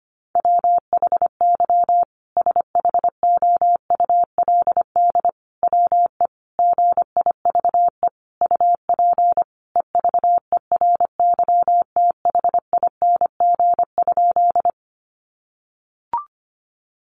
Morse code, heard as a keyed tone, about 25 wpm, 700 Hz. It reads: WHY SHOULD WE GIVE UP EVERYTHING?
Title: Sentences from Top 700 Words 25wpm